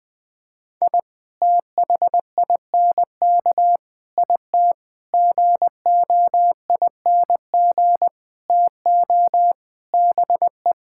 Morse code, heard as a keyed tone, about 21 wpm, 700 Hz. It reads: I THINK IT GOING TO BE
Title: Sentences with 6-Word Phrases - 20wpm